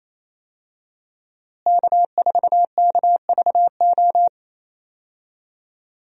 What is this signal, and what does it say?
Morse code, heard as a keyed tone, about 28 wpm, 700 Hz. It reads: K4KVO
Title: USA Call Signs - 5 Characters - Repeated 3x - 28wpm